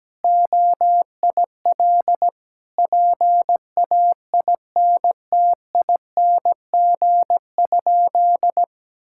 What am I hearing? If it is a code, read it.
Morse code, heard as a keyed tone, about 17 wpm, 700 Hz. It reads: OIL PAINTING?
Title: Sentences from Top 1000 Words - Review Part 6 - 17wpm